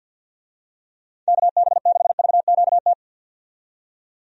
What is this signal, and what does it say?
Morse code, heard as a keyed tone, about 53 wpm, 700 Hz. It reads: KB6VQT
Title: USA Call Signs - 6 Characters - 50wpm